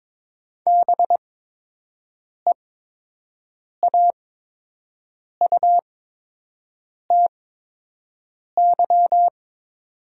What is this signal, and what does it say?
Morse code, heard as a keyed tone, about 22 wpm, spaced as for 6 wpm, 700 Hz. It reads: BEAUTY
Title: 8x Character Spacing - Top 500 Words - 22wpm